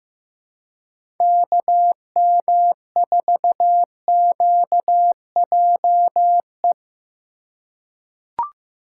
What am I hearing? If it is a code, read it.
Morse code, heard as a keyed tone, about 16 wpm, 700 Hz. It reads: KM4QJE
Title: USA Call Signs - 3x Character Spacing - 15wpm